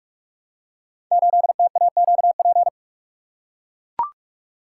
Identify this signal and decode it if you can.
Morse code, heard as a keyed tone, about 44 wpm, 690 Hz. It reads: 8TAQP
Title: Code Groups - 5 Letter:Numbers - 45wpm